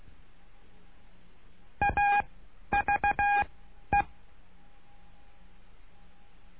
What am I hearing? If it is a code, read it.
Morse code, heard as a keyed tone, about 16 wpm, spaced as for 11 wpm, 810 Hz. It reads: AVE
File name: cursos / CURSO DE TELEGRAFIA / Telegrafia - Trilha 09